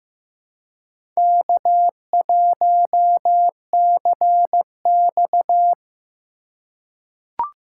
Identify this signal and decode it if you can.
Morse code, heard as a keyed tone, about 16 wpm, 700 Hz. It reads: K1CX
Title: K1USN SST QSO Practice - 15wpm